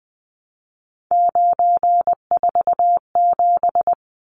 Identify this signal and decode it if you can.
Morse code, heard as a keyed tone, about 20 wpm, 700 Hz. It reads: 947
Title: Numbers - 3-Digit 20wpm